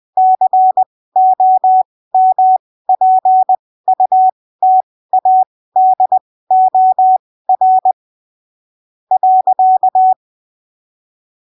Morse code, heard as a keyed tone, about 20 wpm, spaced as for 15 wpm, 750 Hz. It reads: COMPUTADOR .
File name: texto 09 20ppm